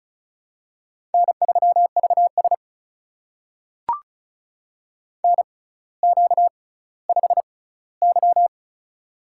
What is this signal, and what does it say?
Morse code, heard as a keyed tone, about 36 wpm, spaced as for 18 wpm, 700 Hz. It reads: N3VS NQ5Y
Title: USA Call Signs - 4 Characters - 6x Character Spacing - 35wpm